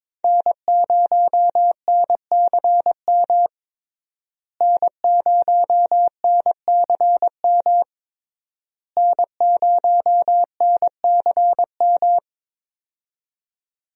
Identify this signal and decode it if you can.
Morse code, heard as a keyed tone, about 22 wpm, 700 Hz. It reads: N0NCM N0NCM N0NCM
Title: USA Call Signs - 5 Characters - Repeated 3x - 22wpm